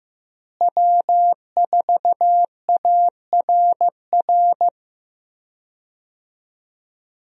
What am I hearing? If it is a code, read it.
Morse code, heard as a keyed tone, about 15 wpm, 700 Hz. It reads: W4ARR
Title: USA Call Signs - 5 Characters - Repeated 3x - 15wpm